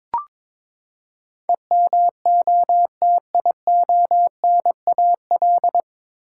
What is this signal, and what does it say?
Morse code, heard as a keyed tone, about 22 wpm, 700 Hz. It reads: EMOTIONAL
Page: Morse Code Ninja practice 22wpm